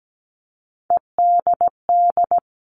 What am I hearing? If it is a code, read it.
Morse code, heard as a keyed tone, about 17 wpm, 700 Hz. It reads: EDD
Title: All Cut Numbers - 3-Digit 17wpm